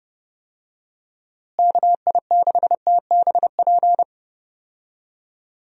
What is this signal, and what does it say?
Morse code, heard as a keyed tone, about 30 wpm, 700 Hz. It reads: KI6TBP
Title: USA Call Signs - 6 Characters - Repeated 2x - 30wpm